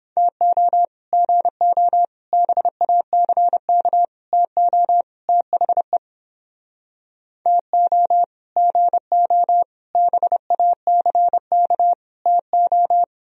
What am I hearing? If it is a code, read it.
Morse code, heard as a keyed tone, about 29 wpm, 700 Hz. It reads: TO GO BACK TO THE TO GO BACK TO